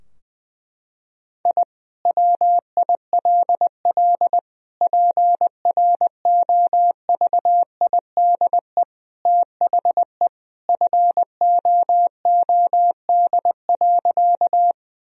Morse code, heard as a keyed tone, about 20 wpm, 700 Hz. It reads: I WILL PROVIDE THE FOOD.
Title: Sentences from Top 1000 Words - Review Part 3 - 20wpm